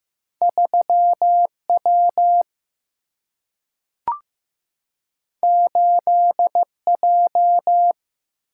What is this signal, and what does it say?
Morse code, heard as a keyed tone, about 15 wpm, 700 Hz. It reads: 3 W 8 J
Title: Single-Letter Single-Number 15wpm